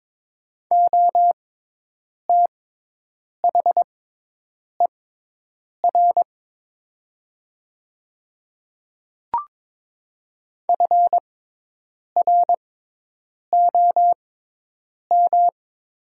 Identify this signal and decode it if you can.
Morse code, heard as a keyed tone, about 22 wpm, spaced as for 8 wpm, 700 Hz. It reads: OTHER FROM
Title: 6x Character Spacing - Top 200 Words - 22wpm